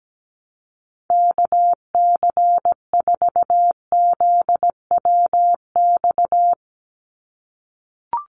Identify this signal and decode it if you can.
Morse code, heard as a keyed tone, about 17 wpm, 700 Hz. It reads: KC4ZWX
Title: USA Call Signs 17wpm